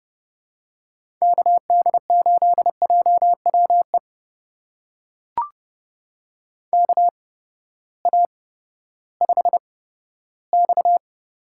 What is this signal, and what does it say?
Morse code, heard as a keyed tone, about 30 wpm, spaced as for 15 wpm, 700 Hz. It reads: KD8JWE KA5X
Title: USA Call Signs - 8x Character Spacing - 30wpm